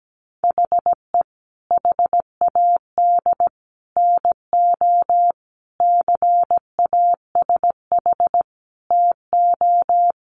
Morse code, heard as a keyed tone, about 17 wpm, 700 Hz. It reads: HE HAD NO CASH TO
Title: Sentences from Top 1100 Words 17wpm